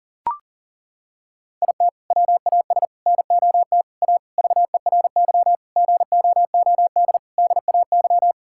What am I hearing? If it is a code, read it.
Morse code, heard as a keyed tone, about 40 wpm, 700 Hz. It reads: IT WAS NOT A VERY GOOD DAY
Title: Sentences from Top 1000 Words - Review Part 6 - 40wpm